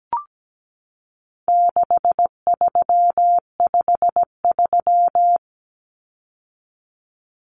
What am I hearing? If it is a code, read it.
Morse code, heard as a keyed tone, about 17 wpm, 700 Hz. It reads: 6353
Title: Numbers - 4-Digit 17wpm